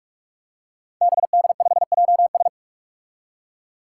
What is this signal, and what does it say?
Morse code, heard as a keyed tone, about 41 wpm, 700 Hz. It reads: BD5JS
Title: Five Challenging Letter-Number Combinations 45wpm